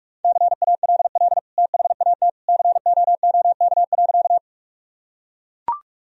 Morse code, heard as a keyed tone, about 41 wpm, 700 Hz. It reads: CALL THAT COOK.
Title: Sentences from Top 1000 Words - Review Part 4 - 45wpm